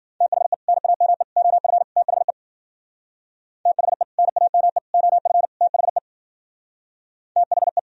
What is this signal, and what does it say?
Morse code, heard as a keyed tone, about 51 wpm, 700 Hz. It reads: THE NAME OF THE THE NAME OF THE THE